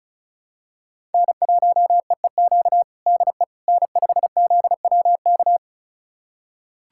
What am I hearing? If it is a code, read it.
Morse code, heard as a keyed tone, about 36 wpm, 700 Hz. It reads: N1EEQ DE N5ZWK
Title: USA Callsign DE Callsign - 5 Characters - 35wpm